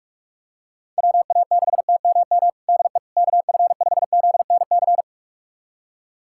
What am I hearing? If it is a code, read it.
Morse code, heard as a keyed tone, about 46 wpm, 700 Hz. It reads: WA6TMM DE KF5ZNC
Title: USA Callsign DE Callsign - 6 Characters - 45wpm